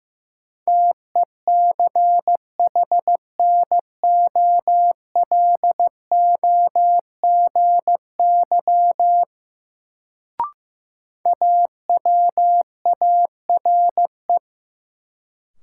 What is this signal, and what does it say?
Morse code, heard as a keyed tone, about 15 wpm, 700 Hz. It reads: TECHNOLOGY AWARE
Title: Sentences from Top 1200 Words - New Word Drill Alt - 15wpm